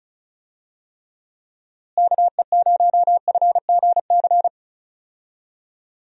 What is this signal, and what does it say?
Morse code, heard as a keyed tone, about 34 wpm, 700 Hz. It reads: KE0FGC
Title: USA Call Signs - 6 Characters - Repeated 3x - 35wpm